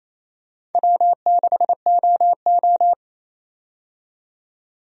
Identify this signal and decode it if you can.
Morse code, heard as a keyed tone, about 27 wpm, 700 Hz. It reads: W6OO
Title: USA Call Signs - 4 Characters - 28wpm